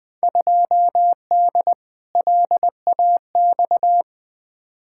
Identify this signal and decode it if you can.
Morse code, heard as a keyed tone, about 21 wpm, 700 Hz. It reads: 2D LAX
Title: Field Day Exchanges - 20wpm